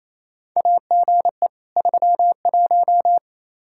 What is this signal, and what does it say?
Morse code, heard as a keyed tone, about 28 wpm, 700 Hz. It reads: AGE 31
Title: Practice QSO - Exchange 3 - More Info - 28wpm